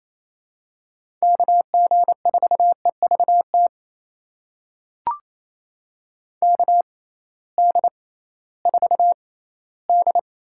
Morse code, heard as a keyed tone, about 28 wpm, spaced as for 16 wpm, 700 Hz. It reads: KG4EVT KD4D
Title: USA Call Signs - 6x Character Spacing - 28wpm